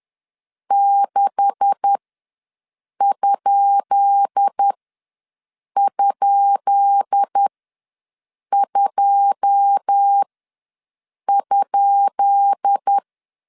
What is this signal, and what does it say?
Morse code, heard as a keyed tone, about 10 wpm, spaced as for 6 wpm, 800 Hz. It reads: 6??2?
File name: LEZIONI / 10 SIGNS / 10B INT ws 10wpm 4134